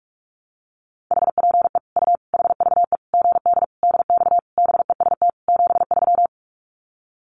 Morse code, heard as a keyed tone, about 44 wpm, 700 Hz. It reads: HPE U HVE GD DX BEST 73
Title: Common QSO Phrases 45wpm